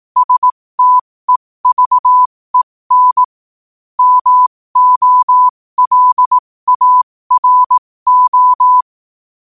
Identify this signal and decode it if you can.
Morse code, heard as a keyed tone, about 18 wpm, spaced as for 15 wpm, 1000 Hz. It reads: STEVEN MOLARO
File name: qotdfiles / 15wpm